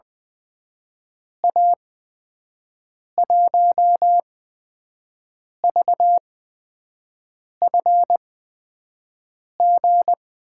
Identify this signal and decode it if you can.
Morse code, heard as a keyed tone, about 20 wpm, spaced as for 5 wpm, 700 Hz. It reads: A1VFG
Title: USA Call Signs - 6 Characters - 8x Character Spacing - 20wpm